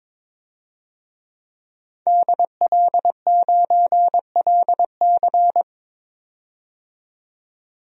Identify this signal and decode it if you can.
Morse code, heard as a keyed tone, about 22 wpm, 700 Hz. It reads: DL9LC
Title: DXCC-100 Call Signs - 5 Characters - 22wpm